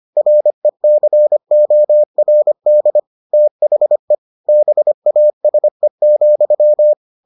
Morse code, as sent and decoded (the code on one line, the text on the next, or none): .-. -.-. --- .-. -.. / - .... / -... .- ... --..--
RECORD THE BASE,